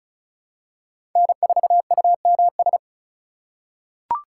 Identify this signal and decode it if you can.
Morse code, heard as a keyed tone, about 36 wpm, 700 Hz. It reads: N4UMS
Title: USA Call Signs - 8x Character Spacing - 35wpm